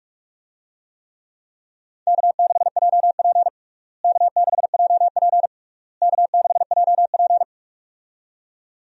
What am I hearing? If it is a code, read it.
Morse code, heard as a keyed tone, about 46 wpm, 700 Hz. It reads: K6JP K6JP K6JP
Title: USA Call Signs - 4 Characters - Repeated 3x - 45wpm